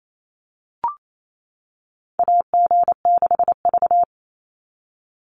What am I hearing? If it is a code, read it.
Morse code, heard as a keyed tone, about 28 wpm, 700 Hz. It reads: AG6V
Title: NA SOTA Chaser Callsigns 28wpm